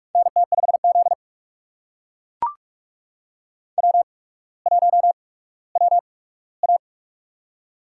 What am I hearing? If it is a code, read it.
Morse code, heard as a keyed tone, about 44 wpm, spaced as for 19 wpm, 690 Hz. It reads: NT5Z W1WA
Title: USA Call Signs - 4 Characters - 8x Character Spacing - 45wpm